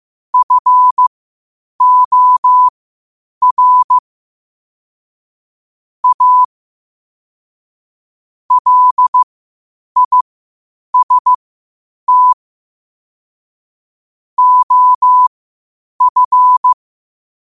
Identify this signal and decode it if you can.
Morse code, heard as a keyed tone, about 15 wpm, spaced as for 8 wpm, 1000 Hz. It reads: FOR A LIST OF
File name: qotdfiles / 8wpm